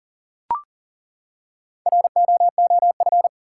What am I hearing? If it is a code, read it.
Morse code, heard as a keyed tone, about 40 wpm, 700 Hz. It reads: ROOF
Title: Four Letter Words - 40wpm